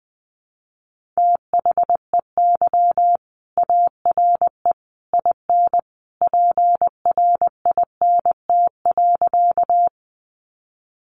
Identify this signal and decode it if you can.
Morse code, heard as a keyed tone, about 20 wpm, 700 Hz. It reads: THEY ARE IN PRINT.
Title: Sentences from Top 1400 Words 20wpm